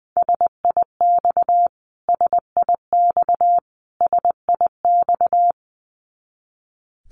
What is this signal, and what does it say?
Morse code, heard as a keyed tone, about 20 wpm, 700 Hz. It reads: SIX SIX SIX
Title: Top 500 Words Repeated 20wpm